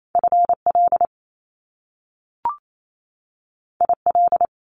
Morse code, heard as a keyed tone, about 28 wpm, 700 Hz. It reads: FL IL
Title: Top Two Letter Combinations in Words 28wpm